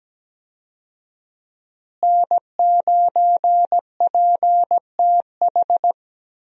Morse code, heard as a keyed tone, about 17 wpm, 700 Hz. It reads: N9PTH
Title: USA Call Signs - 5 Characters - Repeated 2x - 17wpm